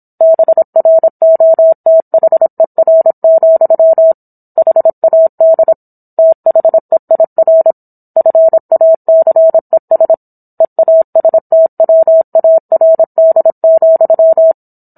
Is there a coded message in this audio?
Yes: Morse code from 0.2 to 14.5 s, heard as a keyed tone, about 27 wpm, 650 Hz. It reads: BROTHER, HAD THEIR FACES EASTWARD,